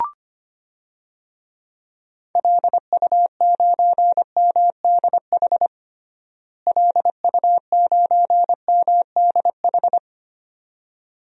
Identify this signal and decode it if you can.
Morse code, heard as a keyed tone, about 25 wpm, 700 Hz. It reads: LU9MDH LU9MDH
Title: DXCC-100 Call Signs - 6 Characters - Repeated 2x - 25wpm